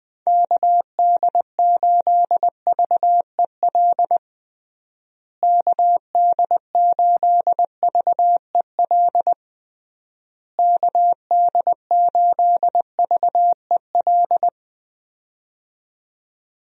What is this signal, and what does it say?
Morse code, heard as a keyed tone, about 20 wpm, 700 Hz. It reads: KD8VEL KD8VEL KD8VEL
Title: USA Call Signs - 6 Characters - Repeated 3x - 20wpm